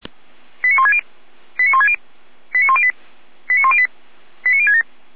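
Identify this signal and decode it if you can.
Five-tone selective call (ZVEI), heard as quick runs of five tones, tones of about 70 ms. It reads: 89160 89160 89108 89108 89067
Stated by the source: ZVEI five-tone selective call 89150 received by SDR.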